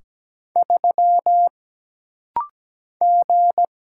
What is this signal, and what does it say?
Morse code, heard as a keyed tone, about 17 wpm, 700 Hz. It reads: 3 G
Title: Single Letter-Number - Rapid-Fire 17wpm